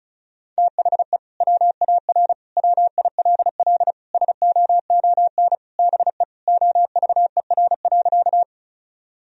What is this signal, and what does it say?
Morse code, heard as a keyed tone, about 36 wpm, 700 Hz. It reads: THE WAR WILL SOON BE OVER.
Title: Sentences from Top 1000 Words - Review Part 2 - 35wpm